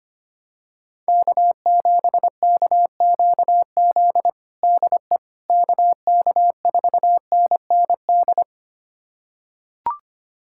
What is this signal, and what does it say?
Morse code, heard as a keyed tone, about 25 wpm, 700 Hz. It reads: K7KQZ DE KK4NND
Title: USA Callsign DE Callsign 25wpm